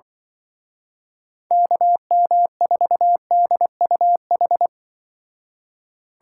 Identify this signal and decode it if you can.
Morse code, heard as a keyed tone, about 24 wpm, 700 Hz. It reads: KM4DUH
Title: Speed Racing - USA 2x3 Call Signs 20wpm